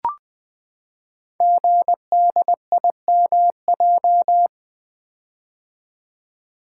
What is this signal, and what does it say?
Morse code, heard as a keyed tone, about 20 wpm, 700 Hz. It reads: GDIMJ
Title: Code Groups - 5 Letters - 20wpm